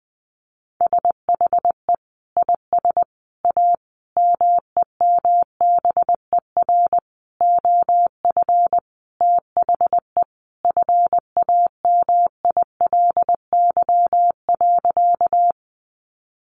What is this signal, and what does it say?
Morse code, heard as a keyed tone, about 20 wpm, 700 Hz. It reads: SHE IS A MEMBER OF THE FAMILY.
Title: Sentences from Top 1100 Words 20wpm